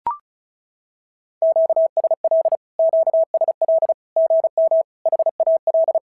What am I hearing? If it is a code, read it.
Morse code, heard as a keyed tone, about 36 wpm, 650 Hz. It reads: QSL QSL GM HAL
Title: SOTA QSO Practice 35wpm